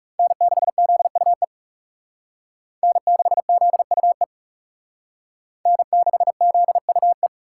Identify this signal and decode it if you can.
Morse code, heard as a keyed tone, about 38 wpm, 700 Hz. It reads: N6ZUE N6ZUE N6ZUE
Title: Speed Racing - USA Call Signs 30wpm